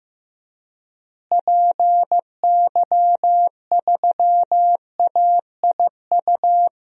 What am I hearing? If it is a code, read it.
Morse code, heard as a keyed tone, about 15 wpm, 700 Hz. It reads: PY3AIU
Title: DXCC-100 Call Signs - 6 Characters - Repeated 3x - 15wpm